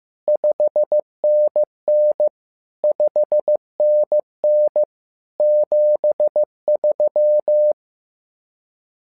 Morse code, heard as a keyed tone, about 15 wpm, 600 Hz. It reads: 5NN 5NN 73
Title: SOTA QSO Practice 15wpm